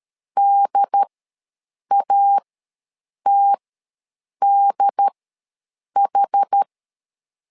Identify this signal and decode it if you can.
Morse code, heard as a keyed tone, about 13 wpm, spaced as for 7 wpm, 800 Hz. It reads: DATDH
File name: LEZIONI / 2 UHDN / 2 UHD ws 12wpm 1722